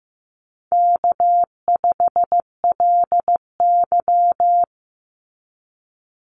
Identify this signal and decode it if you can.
Morse code, heard as a keyed tone, about 15 wpm, 700 Hz. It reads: K5LY
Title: Easy Call Signs Repeated 3x - 15wpm